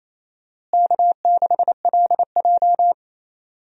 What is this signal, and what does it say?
Morse code, heard as a keyed tone, about 28 wpm, 700 Hz. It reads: K6LJ
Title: USA Call Signs - 4 Characters - Repeated 3x - 28wpm